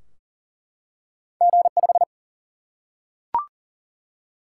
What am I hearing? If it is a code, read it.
Morse code, heard as a keyed tone, about 42 wpm, 700 Hz. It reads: G5
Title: Single-Letter Single-Number 40wpm